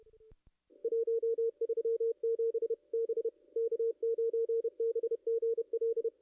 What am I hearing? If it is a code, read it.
Morse code, heard as a keyed tone, about 30 wpm, 440 Hz, with radio noise. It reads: 137 B K9BGL
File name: contests / ss-cw-k5tr-op-n5rz -2300